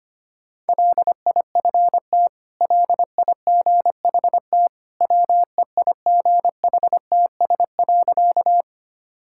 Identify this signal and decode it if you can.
Morse code, heard as a keyed tone, about 25 wpm, 700 Hz. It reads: LIFT LIGHT WEIGHTS.
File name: Easy Sentences with Hints - 25wpm